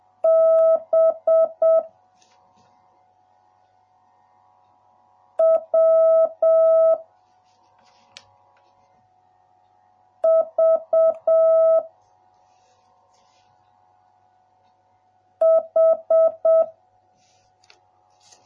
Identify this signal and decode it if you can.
Morse code, heard as a keyed tone, about 7 wpm, 1300 Hz. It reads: B W V H